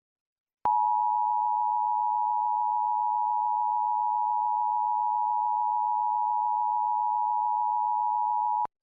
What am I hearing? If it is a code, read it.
Emergency Alert System (SAME) tones (the 853 and 960 Hz attention tone). It is heard from 0.7 to 8.3 s.